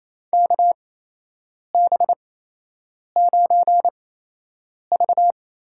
Morse code, heard as a keyed tone, about 27 wpm, spaced as for 8 wpm, 700 Hz. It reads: KB9V
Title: USA Call Signs - 6 Characters - 8x Character Spacing - 28wpm